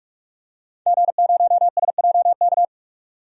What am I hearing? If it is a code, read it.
Morse code, heard as a keyed tone, about 46 wpm, 700 Hz. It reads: G0SJK
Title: DXCC-100 Call Signs - 5 Characters - 45wpm